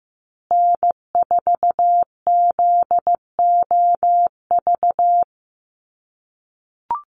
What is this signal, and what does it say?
Morse code, heard as a keyed tone, about 15 wpm, 700 Hz. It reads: N4ZOV
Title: USA Call Signs Repeated 3x 15wpm